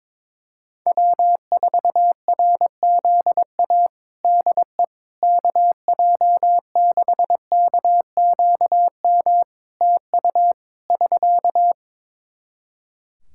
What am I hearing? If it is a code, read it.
Morse code, heard as a keyed tone, about 22 wpm, 700 Hz. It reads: W4RZA DE KJ6KQM TU <SK>
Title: Practice QSO - Exchange 4 - Ending - 22wpm